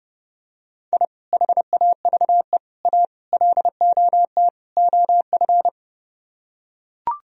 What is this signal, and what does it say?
Morse code, heard as a keyed tone, about 30 wpm, 700 Hz. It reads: I HAVE A LOT OF